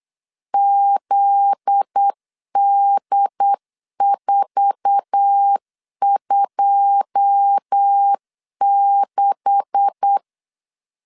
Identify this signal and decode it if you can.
Morse code, heard as a keyed tone, about 8 wpm, 800 Hz. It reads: ZD426
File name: CW / DET / 1 9 1500C S A 8wpm 5637